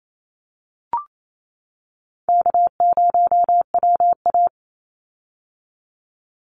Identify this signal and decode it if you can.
Morse code, heard as a keyed tone, about 28 wpm, 700 Hz. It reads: K0WA
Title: Call Signs - 28wpm